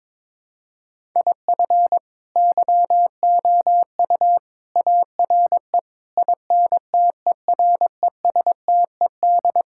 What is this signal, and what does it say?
Morse code, heard as a keyed tone, about 22 wpm, 700 Hz. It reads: IF YOU ARE INTERESTED
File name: practice / 22wpm / Sets of 8 Words - 22wpm